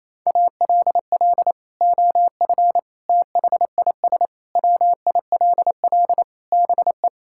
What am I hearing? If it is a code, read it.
Morse code, heard as a keyed tone, about 28 wpm, 700 Hz. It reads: ALL OF THIS WILL BE
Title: Sentences from Top 1000 Words - Review Part 6 - 28wpm